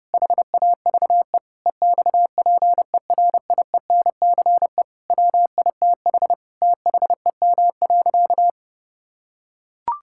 Morse code, heard as a keyed tone, about 32 wpm, 700 Hz. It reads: HAVE EXPERIENCE WITH THEM.
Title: Sentences from Top 1000 Words - Review Part 6 - 30wpm